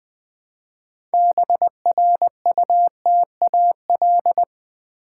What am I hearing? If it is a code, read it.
Morse code, heard as a keyed tone, about 20 wpm, 690 Hz. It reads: BRUTAL
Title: Build A Word - 2 Letters At A Time - 6-Letter Words - 20wpm